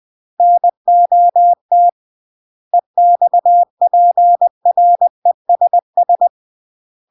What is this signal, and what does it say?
Morse code, heard as a keyed tone, about 20 wpm, 700 Hz. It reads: NOT EXPRESS
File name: CW / jodorowsky / 82 commandments 20-20